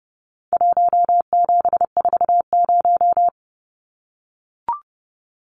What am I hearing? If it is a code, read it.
Morse code, heard as a keyed tone, about 30 wpm, 700 Hz. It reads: 1740
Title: Numbers - 4-Digit 30wpm